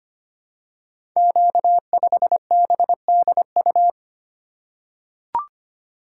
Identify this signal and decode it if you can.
Morse code, heard as a keyed tone, about 25 wpm, 700 Hz. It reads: Q5BDU
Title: Code Groups - 5 Letter:Numbers - 25wpm